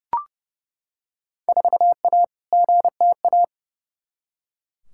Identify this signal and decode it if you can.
Morse code, heard as a keyed tone, about 30 wpm, 700 Hz. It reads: 4A GTA
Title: Field Day Exchanges Encore - 30wpm